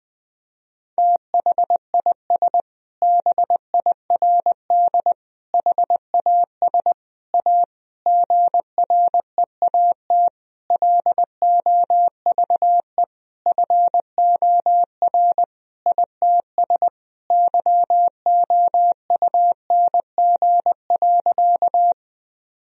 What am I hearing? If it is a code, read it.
Morse code, heard as a keyed tone, about 20 wpm, 700 Hz. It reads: THIS BIRD HAS A GREAT LOVE FOR ITS YOUNG.
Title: Sentences from Top 1000 Words - Review Part 2 - 20wpm